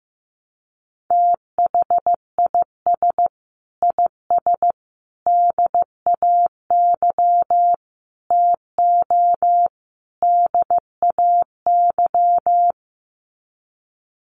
Morse code, heard as a keyed tone, about 15 wpm, 700 Hz. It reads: THIS IS DAY TO DAY
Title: Sets of 5 Words Encore 15wpm